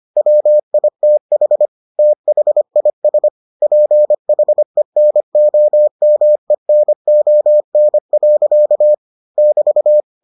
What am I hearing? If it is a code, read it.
Morse code, heard as a keyed tone, about 25 wpm, 600 Hz. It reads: WITH THIS PHENOMENON. <BT>